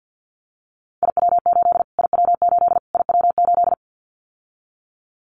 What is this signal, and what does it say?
Morse code, heard as a keyed tone, about 50 wpm, 700 Hz. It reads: IW8 IW8 IW8
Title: Callsign Prefixes for Top 25 DXCC - Repeated 3x - 50wpm